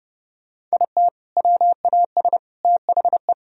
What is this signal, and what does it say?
Morse code, heard as a keyed tone, about 39 wpm, 700 Hz. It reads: IT WAS THE